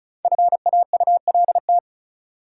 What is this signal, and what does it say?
Morse code, heard as a keyed tone, about 36 wpm, 700 Hz. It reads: FAULT
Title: Five Letter Words - 35wpm